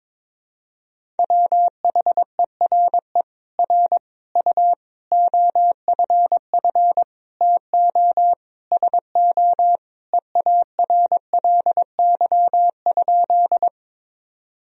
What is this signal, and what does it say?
Morse code, heard as a keyed tone, about 22 wpm, 700 Hz. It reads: WHERE R U OFF TO SO EARLY?
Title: Sentences from Top 400 Words and Abbreviations 22wpm